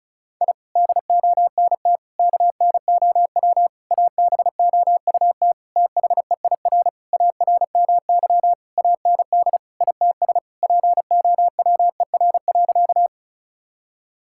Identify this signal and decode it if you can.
Morse code, heard as a keyed tone, about 36 wpm, 700 Hz. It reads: I DONT KNOW ABOUT THEIR ARMY AND ITS POWER.
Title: Sentences from Top 1000 Words - Review Part 5 - 35wpm